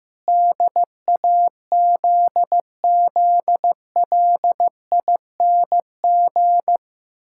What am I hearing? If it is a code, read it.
Morse code, heard as a keyed tone, about 15 wpm, 700 Hz. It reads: DAZZLING
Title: Pangram Challenge v2 - 15wpm